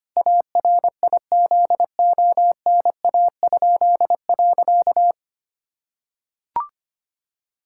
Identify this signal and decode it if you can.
Morse code, heard as a keyed tone, about 25 wpm, 700 Hz. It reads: ARIZONA?.
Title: Pangram Challenge v2 - 25wpm